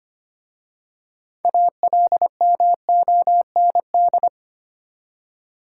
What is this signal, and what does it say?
Morse code, heard as a keyed tone, about 25 wpm, 700 Hz. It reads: ALMOND
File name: practice / 25wpm / Easy Words with Hints - 25wpm